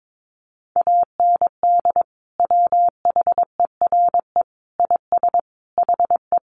Morse code, heard as a keyed tone, about 22 wpm, 700 Hz. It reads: AND WHERE IS HE
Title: Sentences from Top 200 Words 22wpm